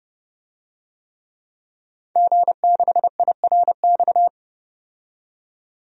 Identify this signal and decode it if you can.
Morse code, heard as a keyed tone, about 30 wpm, 700 Hz. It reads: G6IRX
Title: DXCC-100 Call Signs - 5 Characters - 30wpm